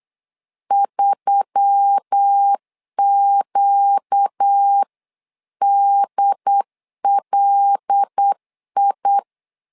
Morse code, heard as a keyed tone, about 8 wpm, 800 Hz. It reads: 3QDLI
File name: LEZIONI / 5 LBVC / 5 LBV 8wpm 2837